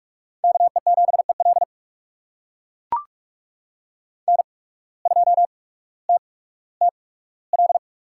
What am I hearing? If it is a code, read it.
Morse code, heard as a keyed tone, about 44 wpm, spaced as for 19 wpm, 700 Hz. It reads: KE7EL N2TTL
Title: USA Call Signs - 5 Characters - 8x Character Spacing - 45wpm